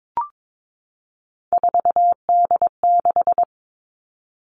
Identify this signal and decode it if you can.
Morse code, heard as a keyed tone, about 22 wpm, 700 Hz. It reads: 4D6
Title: All Cut Numbers - 3-Digit 22wpm